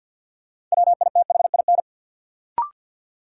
Morse code, heard as a keyed tone, about 51 wpm, 700 Hz. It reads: WITHIN
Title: Sets of 2 Words - New Word Drill - 50wpm